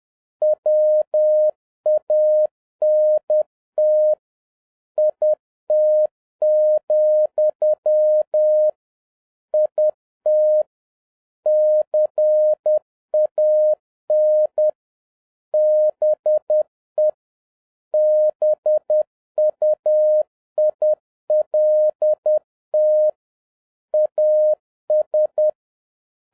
Morse code, heard as a keyed tone, about 10 wpm, 610 Hz. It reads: WANT IT, IT CAN BE BUILT AS